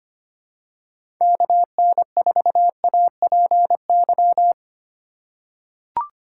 Morse code, heard as a keyed tone, about 25 wpm, 700 Hz. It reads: KN4APY
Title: USA Call Signs - 2x Character Spacing - 25wpm